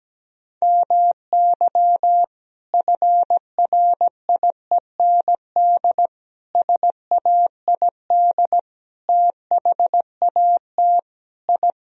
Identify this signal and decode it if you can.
Morse code, heard as a keyed tone, about 17 wpm, 700 Hz. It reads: MY FRIEND SAID THAT I
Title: Sentences from Top 1000 Words - Review Part 4 - 17wpm